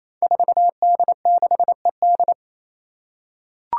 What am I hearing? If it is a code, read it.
Morse code, heard as a keyed tone, about 27 wpm, 700 Hz. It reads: 4D6ED
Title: Five Challenging Letter-Number Combinations 28wpm